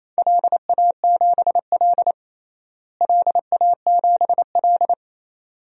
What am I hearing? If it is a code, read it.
Morse code, heard as a keyed tone, about 28 wpm, 700 Hz. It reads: LA7L LA7L
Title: DXCC-100 Call Signs - 4 Characters - Repeated 2x - 28wpm